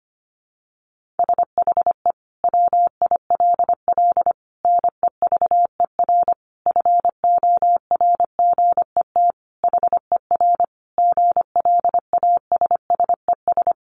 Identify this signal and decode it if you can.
Morse code, heard as a keyed tone, about 25 wpm, 700 Hz. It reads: SHE WILL NEVER FORGET HER GLASSES
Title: Additional Easy Phrases 25wpm